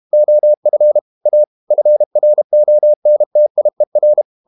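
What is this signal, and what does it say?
Morse code, heard as a keyed tone, about 32 wpm, 600 Hz. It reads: OF A FRONTIER
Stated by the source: Morse code practice at 32 wpm from CW med Gnister, reading Ken Follett's novel Never in English.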